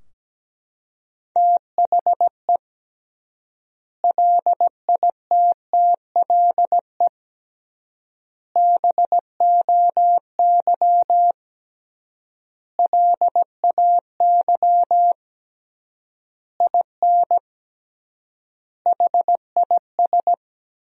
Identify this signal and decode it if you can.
Morse code, heard as a keyed tone, about 17 wpm, 700 Hz. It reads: THE LITTLE BOY LAY IN HIS
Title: 3x Word-Spacing - Sentences from Top 500 Words - 17wpm